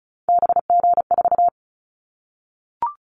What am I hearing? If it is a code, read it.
Morse code, heard as a keyed tone, about 36 wpm, 700 Hz. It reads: BG4
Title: Callsign Prefixes for Top 100 DXCC - 35wpm